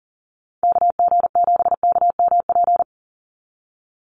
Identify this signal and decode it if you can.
Morse code, heard as a keyed tone, about 40 wpm, 700 Hz. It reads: KG7KMP
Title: USA Call Signs 40wpm